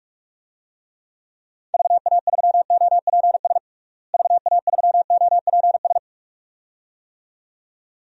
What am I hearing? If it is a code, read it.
Morse code, heard as a keyed tone, about 46 wpm, 700 Hz. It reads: VA3OPS VA3OPS
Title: DXCC-100 Call Signs - 6 Characters - Repeated 2x - 45wpm